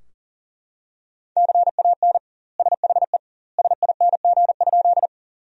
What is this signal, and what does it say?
Morse code, heard as a keyed tone, about 40 wpm, 700 Hz. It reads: CAN SHE SING?
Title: 2x Word-Spacing - Sentences from Top 400 Words - 40wpm